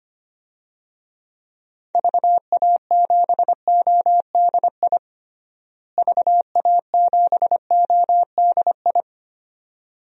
Morse code, heard as a keyed tone, about 25 wpm, 700 Hz. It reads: VA7ODI VA7ODI
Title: DXCC-100 Call Signs - 6 Characters - Repeated 2x - 25wpm